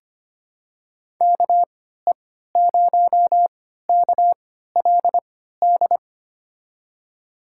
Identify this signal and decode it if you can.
Morse code, heard as a keyed tone, about 25 wpm, spaced as for 14 wpm, 700 Hz. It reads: KE0KLD
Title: USA Call Signs - 6 Characters - 3x Character Spacing - 25wpm